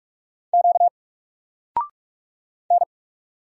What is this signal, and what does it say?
Morse code, heard as a keyed tone, about 46 wpm, 700 Hz. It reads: Q N
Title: Single Letter-Number 45wpm